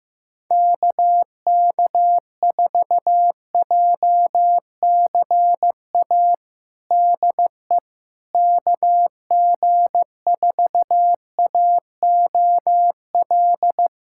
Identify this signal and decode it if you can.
Morse code, heard as a keyed tone, about 15 wpm, 700 Hz. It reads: KK4JCA DE KG4AOL
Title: USA Callsign DE Callsign - 6 Characters - 15wpm